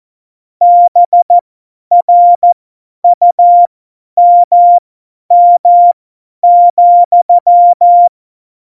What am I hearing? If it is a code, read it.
Morse code, heard as a keyed tone, about 14 wpm, spaced as for 10 wpm, 700 Hz. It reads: BRUMM,
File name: Morsekurs / Witze / Witz 23. Januar 2018, 50 F70